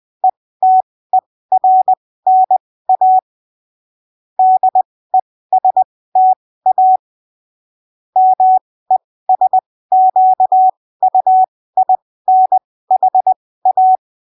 Morse code, heard as a keyed tone, about 20 wpm, spaced as for 15 wpm, 750 Hz. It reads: ETERNA DESTA MESQUINHA
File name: arquivos / texto / texto 27 20ppm